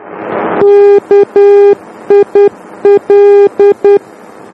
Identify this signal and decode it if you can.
Morse code, heard as a keyed tone, about 10 wpm, 400 Hz. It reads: KIL
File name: germany / mx / KIL-D